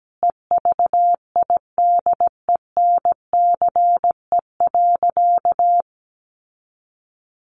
Morse code, heard as a keyed tone, about 17 wpm, 700 Hz. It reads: EVIDENCE.
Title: Sentences from Top 1300 Words 17wpm